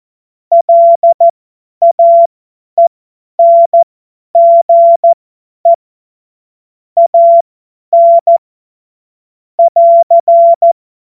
Morse code, heard as a keyed tone, about 14 wpm, spaced as for 10 wpm, 680 Hz. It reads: LAENGE AN <AR>
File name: Morsekurs / Witze / Witz 21. Juni 2022 50 F70